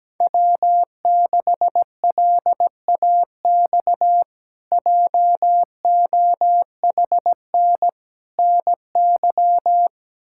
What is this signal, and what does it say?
Morse code, heard as a keyed tone, about 17 wpm, 700 Hz. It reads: W6LAX JOHN NY
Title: K1USN SST QSO Practice - 17wpm